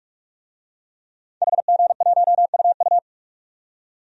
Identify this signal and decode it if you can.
Morse code, heard as a keyed tone, about 46 wpm, 700 Hz. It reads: HG1UU
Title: DXCC-100 Call Signs - 5 Characters - 45wpm